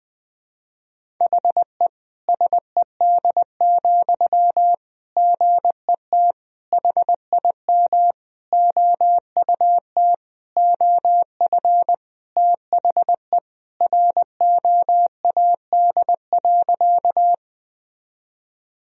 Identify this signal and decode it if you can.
Morse code, heard as a keyed tone, about 20 wpm, 700 Hz. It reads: HE SED, GET HIM OUT OF THE ROAD.
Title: Sentences from Top 400 Words and Abbreviations 20wpm